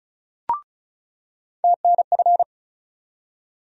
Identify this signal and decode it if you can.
Morse code, heard as a keyed tone, about 36 wpm, 700 Hz. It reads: TNF
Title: USA Call Sign Postfixes - 3 Letters 35wpm